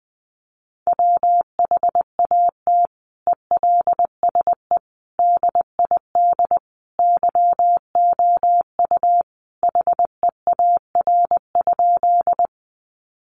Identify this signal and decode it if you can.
Morse code, heard as a keyed tone, about 20 wpm, 700 Hz. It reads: WHAT ELSE DID YOU HEAR?
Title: Sentences from Top 800 Words 20wpm